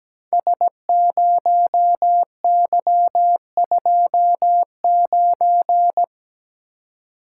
Morse code, heard as a keyed tone, about 17 wpm, 700 Hz. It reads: S0Y29
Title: Code Groups - 5 Letter:Numbers - 17wpm